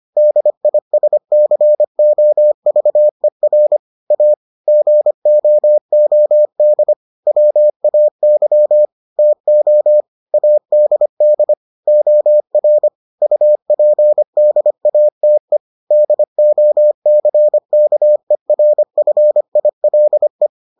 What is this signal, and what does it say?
Morse code, heard as a keyed tone, about 25 wpm, 600 Hz. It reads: DISCOVER A GOOD WAY TO ADD OR UPDATE DOCKERFILE